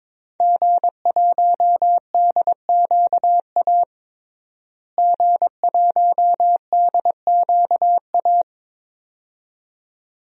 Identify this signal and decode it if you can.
Morse code, heard as a keyed tone, about 22 wpm, 700 Hz. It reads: G1DQA G1DQA
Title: DXCC-100 Call Signs - 5 Characters - Repeated 2x - 22wpm